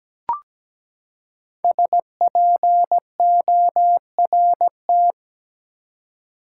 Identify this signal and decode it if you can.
Morse code, heard as a keyed tone, about 17 wpm, 700 Hz. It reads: SPORT
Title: Five Letter Words - 17wpm